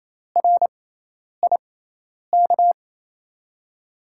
Morse code, heard as a keyed tone, about 27 wpm, spaced as for 10 wpm, 700 Hz. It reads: RIK
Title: USA Call Signs - 6x Character Spacing - 28wpm